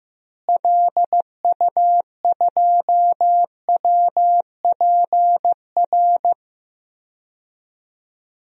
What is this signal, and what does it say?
Morse code, heard as a keyed tone, about 15 wpm, 700 Hz. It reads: LU2WPR
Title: DXCC-100 Call Signs - 6 Characters - Repeated 2x - 15wpm